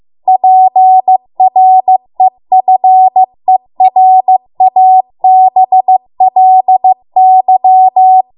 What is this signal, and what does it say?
Morse code, heard as a keyed tone, about 15 wpm, 750 Hz. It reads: PREFERABLY